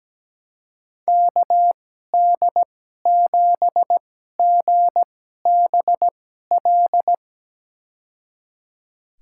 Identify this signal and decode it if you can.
Morse code, heard as a keyed tone, about 17 wpm, spaced as for 12 wpm, 700 Hz. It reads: KD7GBL
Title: USA Call Signs - 6 Characters - 2x Character Spacing - 17wpm